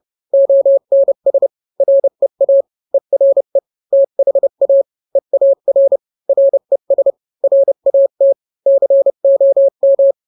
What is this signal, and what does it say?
Morse code, heard as a keyed tone, about 30 wpm, 550 Hz. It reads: ONS REA ERE THA EAR RES RAT COM